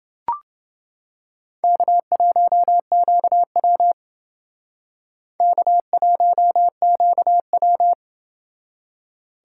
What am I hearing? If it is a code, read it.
Morse code, heard as a keyed tone, about 29 wpm, 700 Hz. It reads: K1QW K1QW